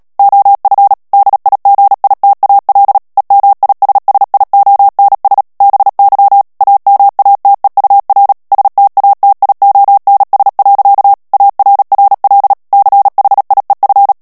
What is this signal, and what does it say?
Morse code, heard as a keyed tone, about 37 wpm, 800 Hz. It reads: OF DIGITAL EMISSIONS BY AMATEUR STATIONS. ARRL CHIEF
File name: morse mp3 / ANGLAIS texte clair 3 vitesse 35 mots minute